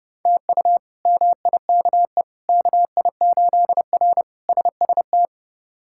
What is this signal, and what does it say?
Morse code, heard as a keyed tone, about 30 wpm, 700 Hz. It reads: TU MIKE KI8R SST
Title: K1USN SST QSO Practice - 30wpm